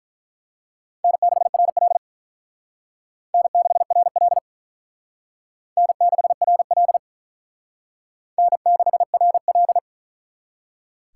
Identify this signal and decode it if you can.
Morse code, heard as a keyed tone, about 44 wpm, 700 Hz. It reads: N6RL N6RL N6RL N6RL
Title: Speed Racing - USA Call Signs 35wpm